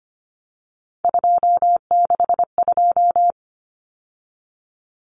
Morse code, heard as a keyed tone, about 25 wpm, 700 Hz. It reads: 262
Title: Numbers - 3-Digit 25wpm